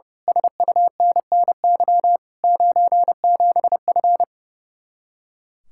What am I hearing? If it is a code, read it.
Morse code, heard as a keyed tone, about 28 wpm, 700 Hz. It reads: SUNNY 97F
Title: Speed-Racing Common QSO Phrases - 30wpm